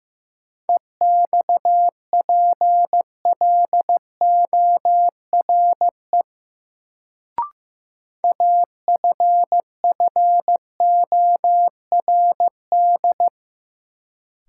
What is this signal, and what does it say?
Morse code, heard as a keyed tone, about 16 wpm, 700 Hz. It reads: EXPLORE AFFORD
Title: Sentences from Top 1100 Words - New Word Drill Alt - 15wpm